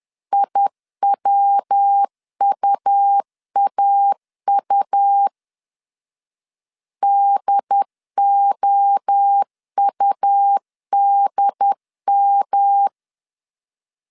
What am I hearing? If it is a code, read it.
Morse code, heard as a keyed tone, about 10 wpm, 800 Hz. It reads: IWUAU DOUDM
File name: CW / DET / 1 3 1500C S B 10wpm 3335